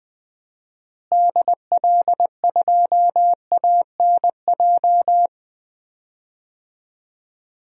Morse code, heard as a keyed tone, about 20 wpm, 700 Hz. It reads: DL2ANJ
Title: DXCC-100 Call Signs - 6 Characters - Repeated 3x - 20wpm